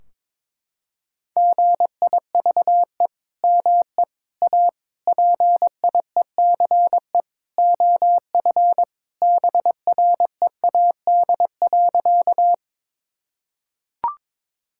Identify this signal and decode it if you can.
Morse code, heard as a keyed tone, about 21 wpm, 700 Hz. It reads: GIVE ME A PIECE OF BREAD.
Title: Sentences from Top 1600 Words 22wpm